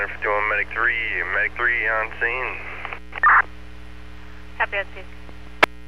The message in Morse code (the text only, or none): none